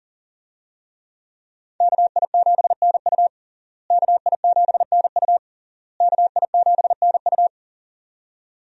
Morse code, heard as a keyed tone, about 40 wpm, 700 Hz. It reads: KI7NU KI7NU KI7NU
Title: USA Call Signs - 5 Characters - Repeated 3x - 40wpm